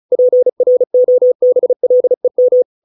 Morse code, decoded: PROBLEM

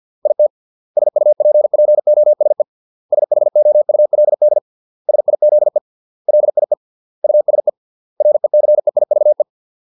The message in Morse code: IT SUPPOSE SHOULD SIZE LIE USE RECEIVE

